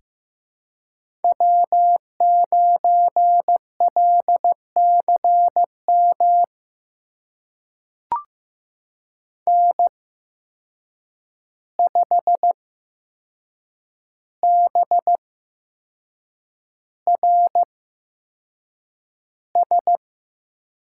Morse code, heard as a keyed tone, about 15 wpm, spaced as for 6 wpm, 700 Hz. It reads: W9LCM N5BRS
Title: USA Call Signs - 5 Characters - 8x Character Spacing - 15wpm